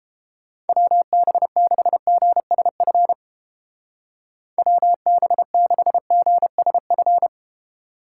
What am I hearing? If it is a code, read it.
Morse code, heard as a keyed tone, about 31 wpm, 700 Hz. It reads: WB6GSF WB6GSF